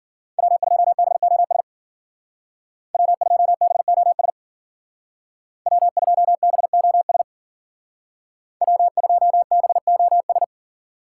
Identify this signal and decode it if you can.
Morse code, heard as a keyed tone, about 48 wpm, 700 Hz. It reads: W2BOS W2BOS W2BOS W2BOS
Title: Speed Racing - USA 1x3 Call Signs 40wpm